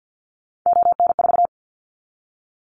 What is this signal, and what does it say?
Morse code, heard as a keyed tone, about 50 wpm, 690 Hz. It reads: ON4
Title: Callsign Prefixes for Top 100 DXCC - 50wpm